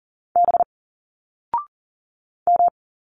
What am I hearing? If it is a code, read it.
Morse code, heard as a keyed tone, about 40 wpm, 700 Hz. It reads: B M
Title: Single Letters 40wpm